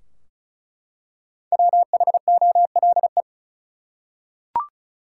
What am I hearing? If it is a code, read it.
Morse code, heard as a keyed tone, about 36 wpm, 700 Hz. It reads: WHOLE
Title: Five Letter Words - 35wpm